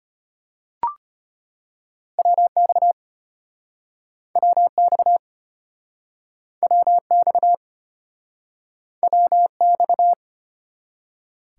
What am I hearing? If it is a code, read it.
Morse code, heard as a keyed tone, about 33 wpm, 700 Hz. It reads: WX WX WX WX